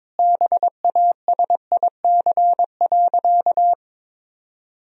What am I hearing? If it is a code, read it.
Morse code, heard as a keyed tone, about 22 wpm, 690 Hz. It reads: BASIC.
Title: Sentences from Top 1000 Words - Review Part 3 - 22wpm